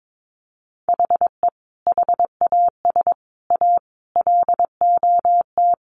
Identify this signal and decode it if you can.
Morse code, heard as a keyed tone, about 22 wpm, 700 Hz. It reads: HE HAS A LOT
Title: Additional Easy Phrases 22wpm